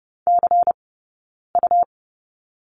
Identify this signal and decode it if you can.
Morse code, heard as a keyed tone, about 30 wpm, spaced as for 9 wpm, 690 Hz. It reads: CU
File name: Lesson 19d - Letter w - Two Characters 30wpm